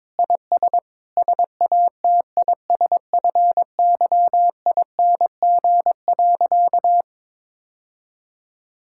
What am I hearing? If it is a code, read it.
Morse code, heard as a keyed tone, about 22 wpm, 700 Hz. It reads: IS SATISFYING.
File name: practice / 22wpm / Sentences with 3-Word Phrases - 22wpm